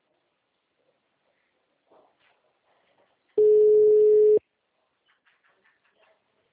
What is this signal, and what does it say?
Ringback tone (Europe, 425 Hz).